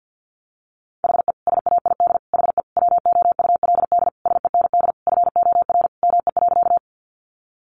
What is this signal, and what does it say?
Morse code, heard as a keyed tone, about 50 wpm, 700 Hz. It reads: HE SAID HE WOULD SEND FOR ME.
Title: Sentences from Top 1500 Words 50wpm